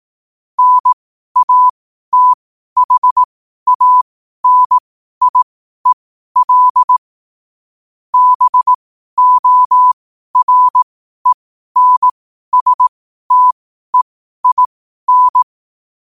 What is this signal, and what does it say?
Morse code, heard as a keyed tone, about 18 wpm, spaced as for 12 wpm, 1000 Hz. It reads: NATHANIEL BORENSTEIN
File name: qotdfiles / 12wpm